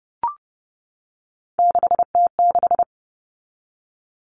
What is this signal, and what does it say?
Morse code, heard as a keyed tone, about 30 wpm, 700 Hz. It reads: 6T6
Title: All Cut Numbers - 3-Digit 30wpm